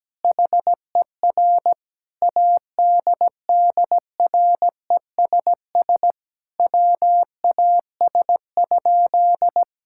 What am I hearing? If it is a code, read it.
Morse code, heard as a keyed tone, about 17 wpm, 700 Hz. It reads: HER ADDRESS WAS?
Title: Sentences from Top 1000 Words - Review Part 6 - 17wpm